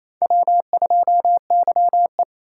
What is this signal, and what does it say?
Morse code, heard as a keyed tone, about 35 wpm, 700 Hz. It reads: W2YE
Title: Speed Racing - USA 1x2 Call Signs 28wpm